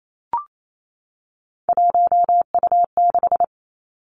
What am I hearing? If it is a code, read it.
Morse code, heard as a keyed tone, about 28 wpm, 700 Hz. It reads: 1U6
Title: Three Challenging Letter-Number Combinations 28wpm